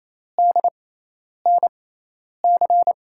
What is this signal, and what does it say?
Morse code, heard as a keyed tone, about 28 wpm, spaced as for 10 wpm, 700 Hz. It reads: DNC
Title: USA Call Signs - 6 Characters - 6x Character Spacing - 28wpm